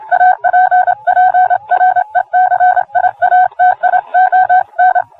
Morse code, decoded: APPRECIATION